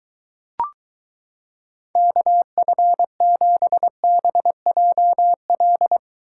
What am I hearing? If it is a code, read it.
Morse code, heard as a keyed tone, about 23 wpm, 700 Hz. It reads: KF7BJL